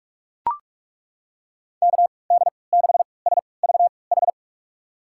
Morse code, heard as a keyed tone, about 42 wpm, spaced as for 25 wpm, 700 Hz. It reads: KD6SVH
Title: USA Call Signs - 3x Character Spacing - 45wpm